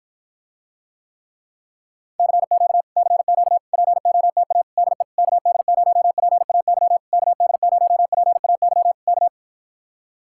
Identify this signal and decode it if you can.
Morse code, heard as a keyed tone, about 54 wpm, 700 Hz. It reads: CQ CQ POTA DE KD0PAY KD0PAY K